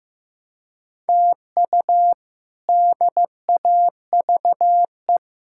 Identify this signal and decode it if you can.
Morse code, heard as a keyed tone, about 15 wpm, 700 Hz. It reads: TU DAVE
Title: K1USN SST QSO Practice - 15wpm